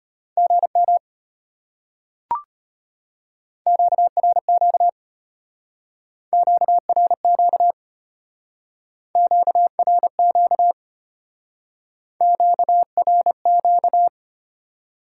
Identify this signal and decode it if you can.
Morse code, heard as a keyed tone, about 31 wpm, 700 Hz. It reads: GM QRQ QRQ QRQ QRQ
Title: Speed-Racing Top 90 QSO Elements 25wpm